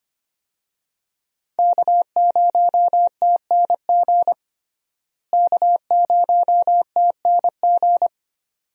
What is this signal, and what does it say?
Morse code, heard as a keyed tone, about 25 wpm, 700 Hz. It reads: K0TNG K0TNG
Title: USA Call Signs - 5 Characters - Repeated 2x - 25wpm